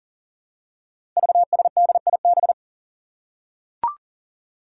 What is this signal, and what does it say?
Morse code, heard as a keyed tone, about 40 wpm, 700 Hz. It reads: VSDIB
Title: Five Challenging Letter-Number Combinations 40wpm